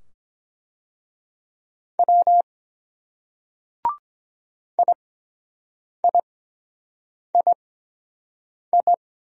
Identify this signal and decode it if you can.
Morse code, heard as a keyed tone, about 23 wpm, spaced as for 7 wpm, 700 Hz. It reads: W IIII